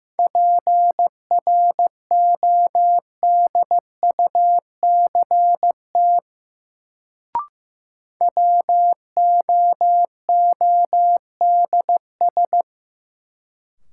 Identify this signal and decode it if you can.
Morse code, heard as a keyed tone, about 15 wpm, 700 Hz. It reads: PRODUCT WOODS
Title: Sentences from Top 1000 Words - Review Part 2 - New Word Drill - 15wpm